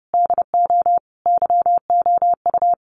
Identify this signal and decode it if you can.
Morse code, heard as a keyed tone, about 30 wpm, 700 Hz. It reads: DO YOU
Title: Sentences from Top 1200 Words 30wpm